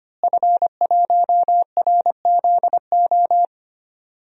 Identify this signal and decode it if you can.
Morse code, heard as a keyed tone, about 25 wpm, 700 Hz. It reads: F1RZO
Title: DXCC-100 Call Signs - 5 Characters - Repeated 2x - 25wpm